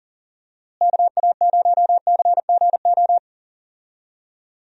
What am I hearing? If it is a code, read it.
Morse code, heard as a keyed tone, about 40 wpm, 700 Hz. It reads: KA0CGO
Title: USA Call Signs - 6 Characters - Repeated 3x - 40wpm